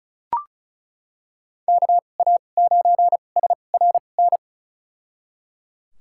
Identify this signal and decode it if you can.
Morse code, heard as a keyed tone, about 34 wpm, spaced as for 25 wpm, 700 Hz. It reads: KA9SRN
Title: USA Call Signs - 6 Characters - 2x Character Spacing - 35wpm